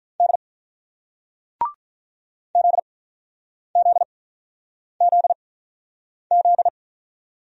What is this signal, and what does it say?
Morse code, heard as a keyed tone, about 47 wpm, spaced as for 9 wpm, 690 Hz. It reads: D ZZZZ